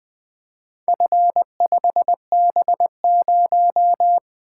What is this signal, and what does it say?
Morse code, heard as a keyed tone, about 20 wpm, 700 Hz. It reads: F5B0
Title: Four Letter-Number Combinations - 20wpm